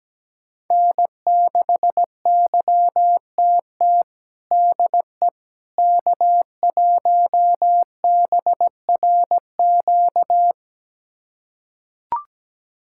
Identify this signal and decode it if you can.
Morse code, heard as a keyed tone, about 17 wpm, 700 Hz. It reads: N6YTT DE K1BRQ
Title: USA Callsign DE Callsign 17wpm